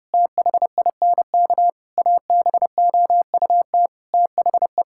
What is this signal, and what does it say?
Morse code, heard as a keyed tone, about 32 wpm, 700 Hz. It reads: THINK ABOUT THE
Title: Sentences from Top 1000 Words - Review Part 5 - 30wpm